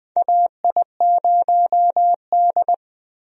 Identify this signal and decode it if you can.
Morse code, heard as a keyed tone, about 20 wpm, 700 Hz. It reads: AI0D
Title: USA Call Signs - 4 Characters - 20wpm